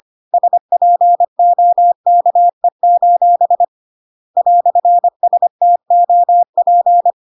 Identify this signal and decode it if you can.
Morse code, heard as a keyed tone, about 25 wpm, 700 Hz. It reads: SPOKE: "STOP
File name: Pinocchio / Pinocchio ch1 25 25 ik1hge